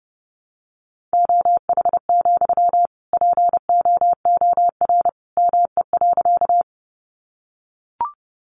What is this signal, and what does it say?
Morse code, heard as a keyed tone, about 30 wpm, 700 Hz. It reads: OH, POOR ME.
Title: Sentences from Top 1000 Words 30wpm